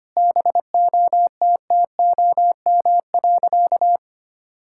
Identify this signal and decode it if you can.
Morse code, heard as a keyed tone, about 25 wpm, 700 Hz. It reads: BOTTOM.
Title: Sentences from Top 1000 Words - Review Part 6 - 25wpm